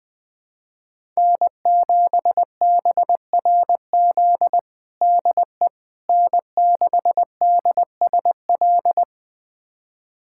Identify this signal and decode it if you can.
Morse code, heard as a keyed tone, about 21 wpm, 700 Hz. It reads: N7BRZ DE N6DSL
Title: USA Callsign DE Callsign - 5 Characters - 20wpm